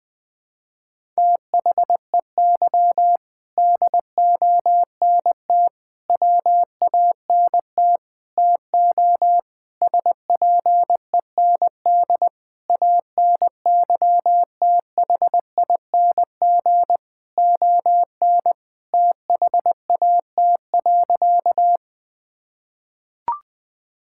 Morse code, heard as a keyed tone, about 20 wpm, 700 Hz. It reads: THEY DONT WANT TO SPEND ANYTHING ON THAT.
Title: Sentences from Top 1000 Words - Review Part 5 - 20wpm